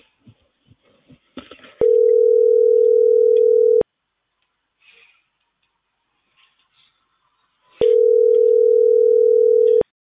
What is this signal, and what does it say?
Ringback tone (North America), 440 + 480 Hz.